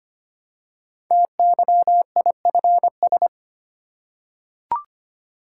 Morse code, heard as a keyed tone, about 25 wpm, 700 Hz. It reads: TYIFS
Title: Code Groups - 5 Letter:Numbers - 25wpm